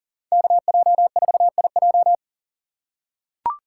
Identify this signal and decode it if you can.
Morse code, heard as a keyed tone, about 40 wpm, 700 Hz. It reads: KJ4IJ
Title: USA Call Signs - 5 Characters - 4x Character Spacing - 40wpm